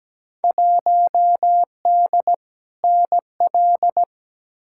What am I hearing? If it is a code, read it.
Morse code, heard as a keyed tone, about 17 wpm, 700 Hz. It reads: 1D NL
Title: Field Day Exchanges Encore - 17wpm